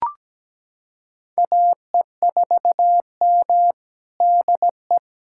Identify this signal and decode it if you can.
Morse code, heard as a keyed tone, about 17 wpm, 700 Hz. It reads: AE4M DE
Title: USA Callsign DE Callsign - 4 Characters - 17wpm